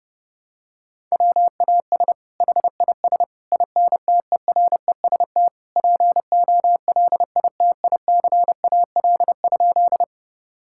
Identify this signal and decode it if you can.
Morse code, heard as a keyed tone, about 32 wpm, 700 Hz. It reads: WAS HIS INTEREST POLITICAL?
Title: Sentences from Top 1000 Words - Review Part 5 - 30wpm